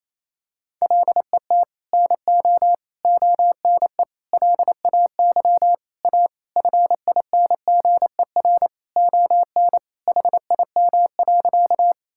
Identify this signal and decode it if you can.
Morse code, heard as a keyed tone, about 28 wpm, 700 Hz. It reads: LET NO ONE LAY A FINGER ON HIM.
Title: Sentences from Top 1600 Words 28wpm